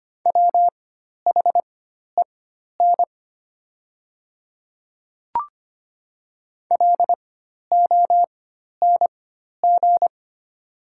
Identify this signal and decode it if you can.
Morse code, heard as a keyed tone, about 25 wpm, spaced as for 12 wpm, 700 Hz. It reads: WHEN LONG
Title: 4x Character Spacing - Top 100 Words - 25wpm